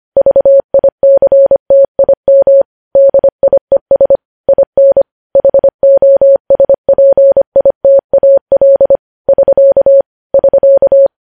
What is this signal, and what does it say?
Morse code, heard as a keyed tone, about 25 wpm, 550 Hz. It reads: VICTIM DIES IN HOSPITAL <SK> <SK>
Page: news-headlines-in-morse-code-at-25-wpm